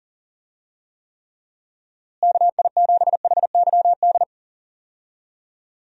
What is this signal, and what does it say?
Morse code, heard as a keyed tone, about 40 wpm, 700 Hz. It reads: KI7HYD
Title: USA Call Signs - 6 Characters - Repeated 3x - 40wpm